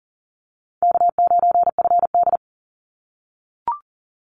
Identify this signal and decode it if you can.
Morse code, heard as a keyed tone, about 40 wpm, 700 Hz. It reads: K9FD
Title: Easy Call Signs - 40wpm